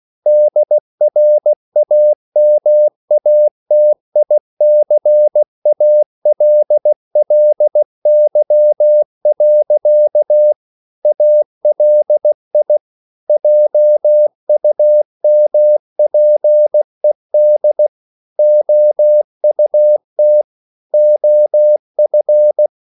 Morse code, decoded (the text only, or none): DRAMATICALLY. ALI JUMPED OUT OF